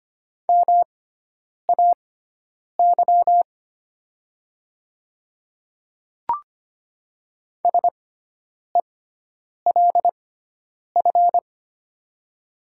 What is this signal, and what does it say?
Morse code, heard as a keyed tone, about 25 wpm, spaced as for 9 wpm, 700 Hz. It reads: MAY SELF
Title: 6x Character Spacing - Top 200 Words - 25wpm